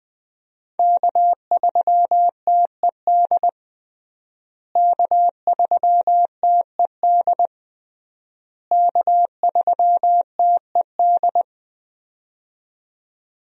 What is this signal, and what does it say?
Morse code, heard as a keyed tone, about 20 wpm, 700 Hz. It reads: K3TED K3TED K3TED
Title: USA Call Signs - 5 Characters - Repeated 3x - 20wpm